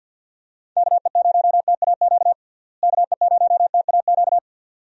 Morse code, heard as a keyed tone, about 49 wpm, 700 Hz. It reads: KE0TAQ KE0TAQ
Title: USA Call Signs - 6 Characters - Repeated 2x - 50wpm